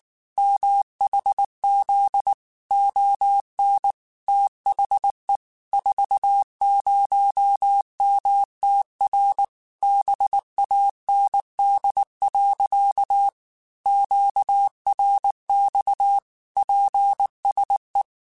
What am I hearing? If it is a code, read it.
Morse code, heard as a keyed tone, about 19 wpm, 800 Hz. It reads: MHZ ON THE 40MTR BAND. QRX PSE